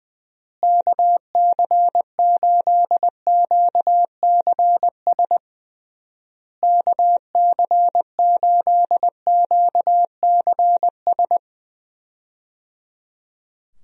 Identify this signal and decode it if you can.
Morse code, heard as a keyed tone, about 20 wpm, 700 Hz. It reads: KC8QCS KC8QCS
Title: USA Call Signs - 6 Characters - Repeated 3x - 20wpm